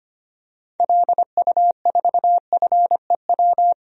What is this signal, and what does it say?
Morse code, heard as a keyed tone, about 25 wpm, 700 Hz. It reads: LU4FEW
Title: DXCC-100 Call Signs - 6 Characters - 25wpm